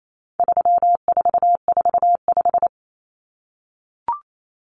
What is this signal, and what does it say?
Morse code, heard as a keyed tone, about 28 wpm, 700 Hz. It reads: 3445
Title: Numbers - 4-Digit 28wpm